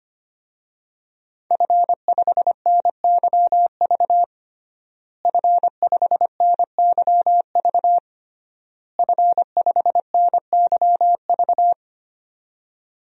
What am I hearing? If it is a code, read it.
Morse code, heard as a keyed tone, about 25 wpm, 700 Hz. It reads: F5NYV F5NYV F5NYV
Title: DXCC-100 Call Signs - 5 Characters - Repeated 3x - 25wpm